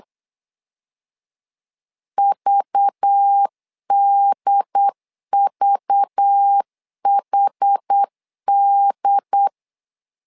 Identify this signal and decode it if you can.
Morse code, heard as a keyed tone, about 8 wpm, 800 Hz. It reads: VDVHD
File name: LEZIONI / 5 LBVC / 5 LBV 8wpm 2837